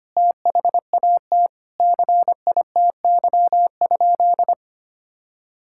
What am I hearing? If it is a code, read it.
Morse code, heard as a keyed tone, about 25 wpm, 700 Hz. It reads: THAT CITY?
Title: Sentences from Top 1000 Words - Review Part 5 - 25wpm